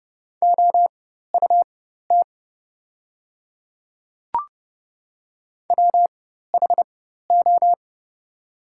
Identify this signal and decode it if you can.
Morse code, heard as a keyed tone, about 30 wpm, spaced as for 14 wpm, 700 Hz. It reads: OUT WHO
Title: 4x Character Spacing - Top 200 Words - 30wpm